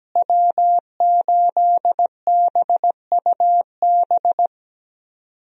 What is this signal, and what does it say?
Morse code, heard as a keyed tone, about 17 wpm, 700 Hz. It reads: W8BUB
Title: USA Call Signs - 5 Characters - Repeated 2x - 17wpm